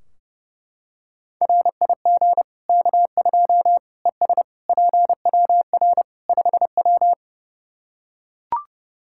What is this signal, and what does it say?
Morse code, heard as a keyed tone, about 30 wpm, 700 Hz. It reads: RIG K2 ES PWR 5W
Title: Practice QSO - Exchange 2 - Info - 30wpm